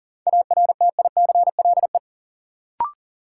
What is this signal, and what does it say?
Morse code, heard as a keyed tone, about 40 wpm, 700 Hz. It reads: ARTICLE
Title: Sentences from Top 1000 Words - Review Part 5 - New Word Drill - 40wpm